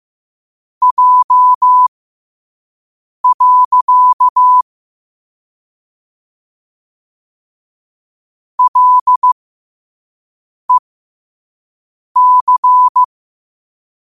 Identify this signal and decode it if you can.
Morse code, heard as a keyed tone, about 15 wpm, spaced as for 5 wpm, 1000 Hz. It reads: J. LEC
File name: qotdfiles / 5wpm